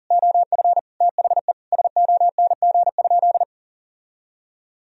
Morse code, heard as a keyed tone, about 40 wpm, 700 Hz. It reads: OF THE SONG?
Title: Sentences from Top 1000 Words - Review Part 2 - 40wpm